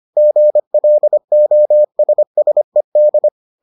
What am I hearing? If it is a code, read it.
Morse code, heard as a keyed tone, about 25 wpm, 600 Hz. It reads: GLOSSED